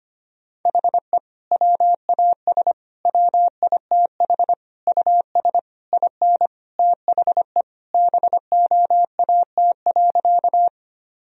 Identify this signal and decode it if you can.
Morse code, heard as a keyed tone, about 25 wpm, 700 Hz. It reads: HE WAS WITH US IN THE BOAT.
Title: Sentences from Top 1000 Words - Review Part 3 - 25wpm